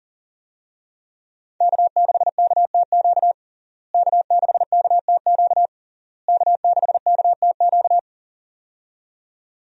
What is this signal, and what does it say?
Morse code, heard as a keyed tone, about 40 wpm, 700 Hz. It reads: K6KTQ K6KTQ K6KTQ
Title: USA Call Signs - 5 Characters - Repeated 3x - 40wpm